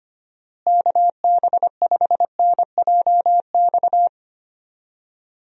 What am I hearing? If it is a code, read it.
Morse code, heard as a keyed tone, about 25 wpm, 700 Hz. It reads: KB5NJX
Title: USA Call Signs - 6 Characters - Repeated 2x - 25wpm